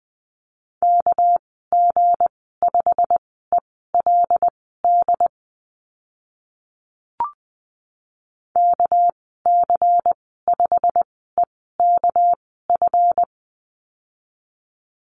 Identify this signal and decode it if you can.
Morse code, heard as a keyed tone, about 20 wpm, spaced as for 14 wpm, 700 Hz. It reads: KG5ELD KC5EKF
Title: 219 Lesson 35g - Number 0 - Call Signs 20wpm